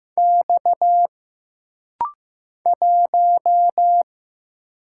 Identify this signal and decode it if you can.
Morse code, heard as a keyed tone, about 15 wpm, 700 Hz. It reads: X 1
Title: Single Letter-Number Alt - Rapid-Fire 15wpm